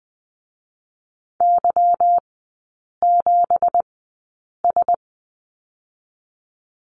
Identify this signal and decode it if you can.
Morse code, heard as a keyed tone, about 20 wpm, 700 Hz. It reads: Y 7 S
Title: 198 Lesson 32e - Letter k - Three Characters 20wpm